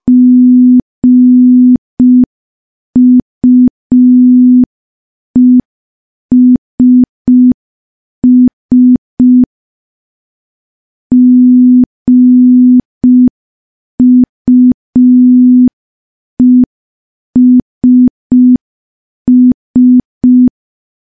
Morse code, heard as a keyed tone, about 5 wpm, 250 Hz. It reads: GUESS GUESS
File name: morse / morse-training-05WPM